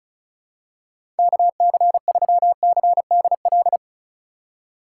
Morse code, heard as a keyed tone, about 34 wpm, 700 Hz. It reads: KC3CDL
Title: USA Call Signs - 6 Characters - 2x Character Spacing - 35wpm